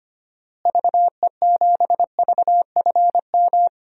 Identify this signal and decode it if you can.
Morse code, heard as a keyed tone, about 25 wpm, 700 Hz. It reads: VE7VFM
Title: DXCC-100 Call Signs - 6 Characters - Repeated 3x - 25wpm